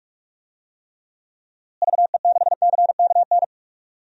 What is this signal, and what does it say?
Morse code, heard as a keyed tone, about 46 wpm, 700 Hz. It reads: VE6CKN
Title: DXCC-100 Call Signs - 6 Characters - Repeated 3x - 45wpm